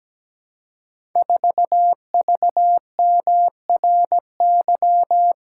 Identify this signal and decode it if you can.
Morse code, heard as a keyed tone, about 17 wpm, 700 Hz. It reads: 4VMRY
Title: Code Groups - 5 Letter:Numbers - 17wpm